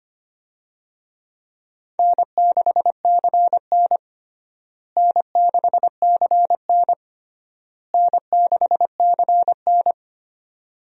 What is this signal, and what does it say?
Morse code, heard as a keyed tone, about 25 wpm, 700 Hz. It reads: N6CN N6CN N6CN
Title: USA Call Signs - 4 Characters - Repeated 3x - 25wpm